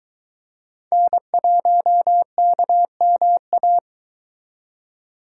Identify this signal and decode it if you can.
Morse code, heard as a keyed tone, about 22 wpm, 700 Hz. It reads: N1KMA